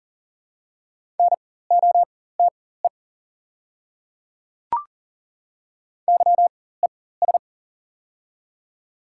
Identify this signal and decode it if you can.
Morse code, heard as a keyed tone, about 40 wpm, spaced as for 19 wpm, 700 Hz. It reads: NOTE YES
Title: 4x Character Spacing - Top 500 Words - 40wpm